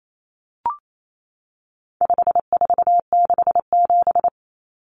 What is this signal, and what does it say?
Morse code, heard as a keyed tone, about 28 wpm, 700 Hz. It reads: 5467
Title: Common Cut Numbers - 4-Digit 28wpm